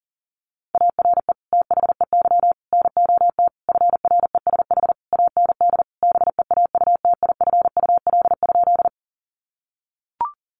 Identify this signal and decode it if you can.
Morse code, heard as a keyed tone, about 40 wpm, 700 Hz. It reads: ARE THEY NOT FRESH AND BEAUTIFUL?
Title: Sentences from Top 1200 Words 40wpm